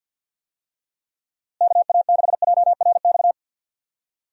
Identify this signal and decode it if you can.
Morse code, heard as a keyed tone, about 45 wpm, 700 Hz. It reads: KA6JRX
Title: USA Call Signs - 6 Characters - Repeated 3x - 50wpm